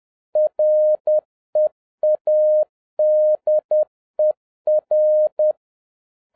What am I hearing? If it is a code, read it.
Morse code, heard as a keyed tone, about 10 wpm, 610 Hz. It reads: READER
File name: 2011-07-mp3 / n46 JOHN K3TN 10wpm